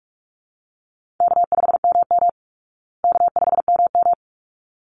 Morse code, heard as a keyed tone, about 44 wpm, 700 Hz. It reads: K5MM K5MM
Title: Easy Call Signs Repeated 2x - 45wpm